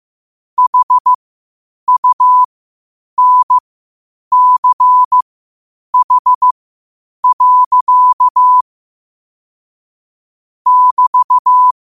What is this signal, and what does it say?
Morse code, heard as a keyed tone, about 15 wpm, spaced as for 8 wpm, 1000 Hz. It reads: HUNCH. <BT>
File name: qotdfiles / 8wpm